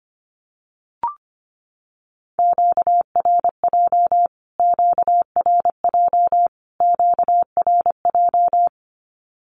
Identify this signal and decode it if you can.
Morse code, heard as a keyed tone, about 25 wpm, 700 Hz. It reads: QRJ QRJ QRJ
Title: Q-Codes Repeated 3x 25wpm